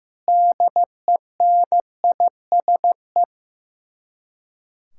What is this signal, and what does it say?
Morse code, heard as a keyed tone, about 15 wpm, 700 Hz. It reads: DENISE
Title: Practice QSO - Exchange 1 - Beginning - 15wpm